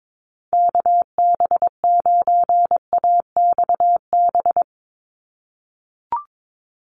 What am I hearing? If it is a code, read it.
Morse code, heard as a keyed tone, about 22 wpm, 700 Hz. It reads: KB9AXB
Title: USA Call Signs 22wpm